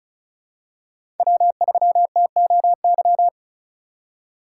Morse code, heard as a keyed tone, about 34 wpm, 700 Hz. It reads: W3TOY
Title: USA Call Signs - 4x Character Spacing - 35wpm